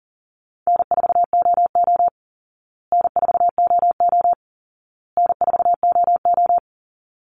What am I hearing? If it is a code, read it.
Morse code, heard as a keyed tone, about 40 wpm, 700 Hz. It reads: N4OO N4OO N4OO
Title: Call Signs Repeated 2x - 40wpm